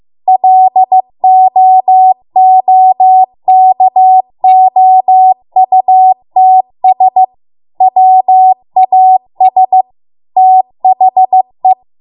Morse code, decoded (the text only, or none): LOOKOUTS WAS THE